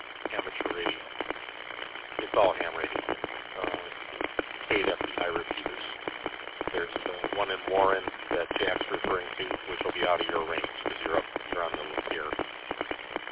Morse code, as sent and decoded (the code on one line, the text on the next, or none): none